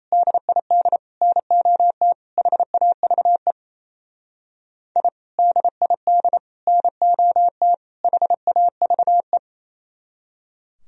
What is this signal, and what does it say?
Morse code, heard as a keyed tone, about 35 wpm, 700 Hz. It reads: DID NOT HAVE I DID NOT HAVE